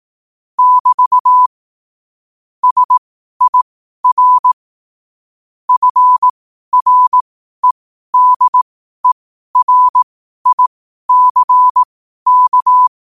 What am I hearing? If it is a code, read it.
Morse code, heard as a keyed tone, about 18 wpm, spaced as for 12 wpm, 1000 Hz. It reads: <BT> SIR FREDERICK
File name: qotdfiles / 12wpm